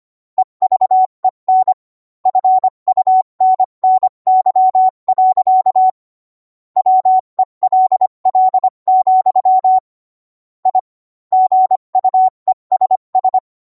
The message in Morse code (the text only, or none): EVEN FUNNY. WELL, I GUESS